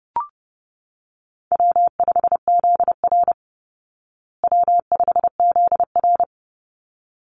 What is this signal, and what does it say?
Morse code, heard as a keyed tone, about 30 wpm, 700 Hz. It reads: W5ZR W5ZR
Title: Easy Call Signs Repeated 2x - 30wpm